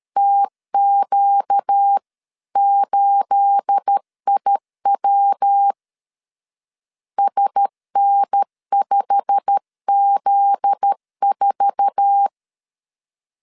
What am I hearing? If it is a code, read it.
Morse code, heard as a keyed tone, about 13 wpm, 800 Hz. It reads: TQ8IW SN5Z4
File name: LEZIONI / 9 2867 12wpm 2322